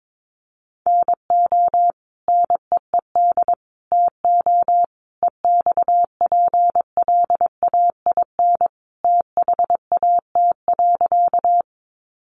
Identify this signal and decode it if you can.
Morse code, heard as a keyed tone, about 22 wpm, 700 Hz. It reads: NO NEED TO EXPLAIN THAT.
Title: Sentences from Top 1100 Words 22wpm